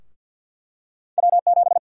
Morse code, heard as a keyed tone, about 53 wpm, 700 Hz. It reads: W7
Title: USA Call Sign Prefixes - Repeated 2x - 50wpm